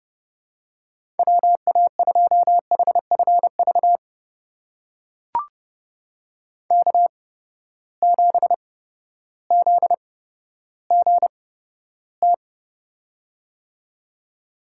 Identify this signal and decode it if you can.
Morse code, heard as a keyed tone, about 30 wpm, spaced as for 14 wpm, 700 Hz. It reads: WA2HFV K7ZGT
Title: USA Call Signs - 8x Character Spacing - 30wpm